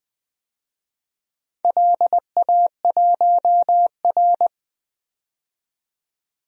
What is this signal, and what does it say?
Morse code, heard as a keyed tone, about 20 wpm, 700 Hz. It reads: LA1R
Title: DXCC-100 Call Signs - 4 Characters - 20wpm